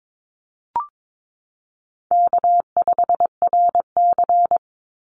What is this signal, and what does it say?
Morse code, heard as a keyed tone, about 22 wpm, 700 Hz. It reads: K5RC
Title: Easy Call Signs Repeated 2x - 22wpm